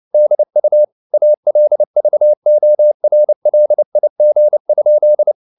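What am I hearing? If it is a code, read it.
Morse code, heard as a keyed tone, about 29 wpm, 600 Hz. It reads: DU ALVORLIG?
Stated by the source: Fast (QRQ) Morse at 29 wpm, from CW med Gnister.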